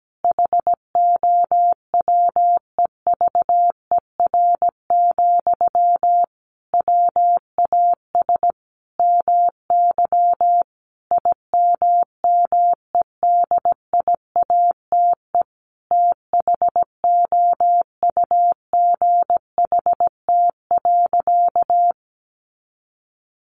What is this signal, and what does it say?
Morse code, heard as a keyed tone, about 17 wpm, 700 Hz. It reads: HOWEVER, WAS MY IMMEDIATE THOUGHT.
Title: Sentences from Top 1500 Words 17wpm